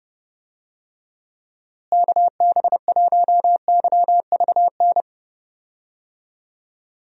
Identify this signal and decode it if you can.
Morse code, heard as a keyed tone, about 32 wpm, 700 Hz. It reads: KB1YVN
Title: USA Call Signs - 6 Characters - Repeated 2x - 30wpm